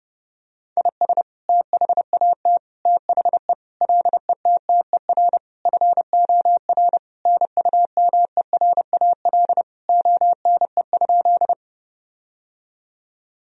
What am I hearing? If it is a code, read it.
Morse code, heard as a keyed tone, about 30 wpm, 700 Hz. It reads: IS THAT THE LETTER FOR NUMERAL ONE?
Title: Sentences from Top 1000 Words - Review Part 2 - 30wpm